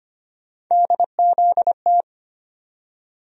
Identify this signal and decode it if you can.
Morse code, heard as a keyed tone, about 25 wpm, 700 Hz. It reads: DZT
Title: USA Call Sign Postfixes - 3 Letters 25wpm